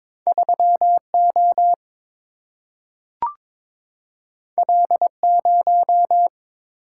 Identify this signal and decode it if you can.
Morse code, heard as a keyed tone, about 22 wpm, 700 Hz. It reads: 3O L0
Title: Single-Letter Single-Number 22wpm